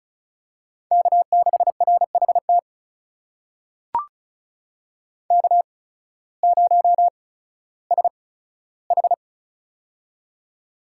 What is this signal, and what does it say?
Morse code, heard as a keyed tone, about 36 wpm, spaced as for 16 wpm, 700 Hz. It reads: K6RHT K0SH
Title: USA Call Signs - 8x Character Spacing - 35wpm